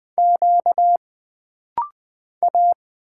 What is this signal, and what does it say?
Morse code, heard as a keyed tone, about 20 wpm, 700 Hz. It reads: Q A
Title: Single Letters - Rapid-Fire 20wpm